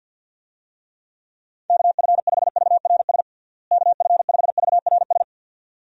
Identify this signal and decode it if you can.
Morse code, heard as a keyed tone, about 51 wpm, 700 Hz. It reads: KF5VRS KF5VRS
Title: USA Call Signs - 6 Characters - Repeated 2x - 50wpm